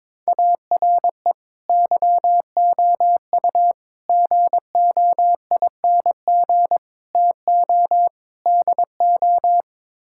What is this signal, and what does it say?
Morse code, heard as a keyed tone, about 22 wpm, 700 Hz. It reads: ARE YOU GOING TO DO
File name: practice / 22wpm / Build A Phrase - 5-Word Phrases - 22wpm